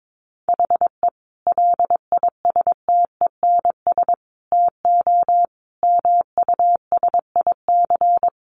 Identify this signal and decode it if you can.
Morse code, heard as a keyed tone, about 22 wpm, 700 Hz. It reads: HE LISTENS TO MUSIC
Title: Additional Easy Phrases 22wpm